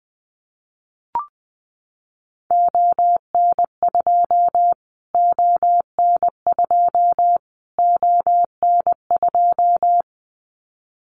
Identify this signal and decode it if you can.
Morse code, heard as a keyed tone, about 20 wpm, 700 Hz. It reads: ON2 ON2 ON2
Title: Callsign Prefixes for Top 25 DXCC - Repeated 3x - 20wpm